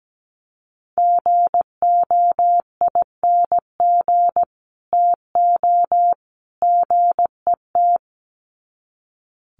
Morse code, heard as a keyed tone, about 17 wpm, 700 Hz. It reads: GOING TO GET
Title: Sets of 3 Words 17wpm